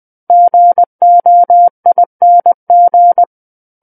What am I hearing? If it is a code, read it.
Morse code, heard as a keyed tone, about 20 wpm, 700 Hz. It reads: GOING
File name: 20 WPM